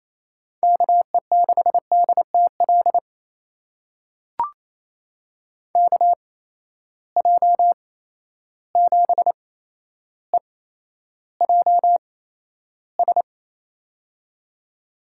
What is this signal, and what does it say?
Morse code, heard as a keyed tone, about 28 wpm, spaced as for 12 wpm, 700 Hz. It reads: KE6DTL KJ7EJS
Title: USA Call Signs - 8x Character Spacing - 28wpm